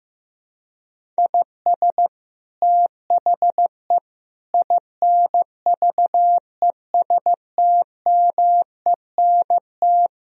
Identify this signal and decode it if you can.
Morse code, heard as a keyed tone, about 15 wpm, 700 Hz. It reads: IS THE INVESTMENT
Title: Sentences from Top 1000 Words - Review Part 4 - 15wpm